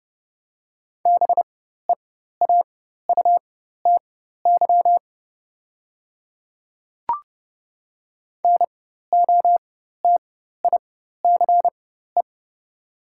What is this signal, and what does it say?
Morse code, heard as a keyed tone, about 29 wpm, spaced as for 14 wpm, 700 Hz. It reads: BEAUTY NOTICE
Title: 4x Character Spacing - Top 500 Words - 30wpm